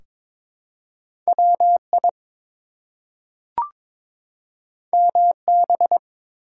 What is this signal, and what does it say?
Morse code, heard as a keyed tone, about 22 wpm, 700 Hz. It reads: WI MB
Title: US States and Canadian Provinces 22wpm